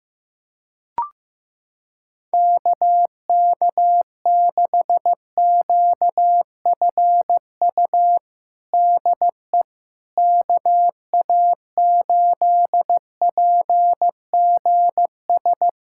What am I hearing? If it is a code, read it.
Morse code, heard as a keyed tone, about 15 wpm, 700 Hz. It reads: KK6QFU DE KA8PGS
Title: USA Callsign DE Callsign - 6 Characters - 15wpm